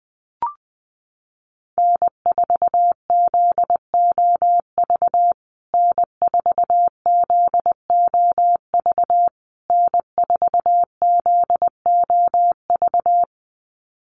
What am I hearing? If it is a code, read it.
Morse code, heard as a keyed tone, about 20 wpm, 700 Hz. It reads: N4ZOV N4ZOV N4ZOV
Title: USA Call Signs Repeated 3x 20wpm